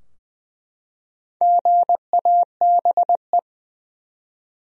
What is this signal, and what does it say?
Morse code, heard as a keyed tone, about 21 wpm, 700 Hz. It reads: GABE
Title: USA Common Names - Max 5 Letters - 20wpm